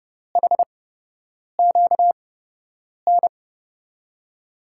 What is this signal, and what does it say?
Morse code, heard as a keyed tone, about 30 wpm, spaced as for 8 wpm, 700 Hz. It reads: HQN
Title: USA Call Signs - 5 Characters - 8x Character Spacing - 30wpm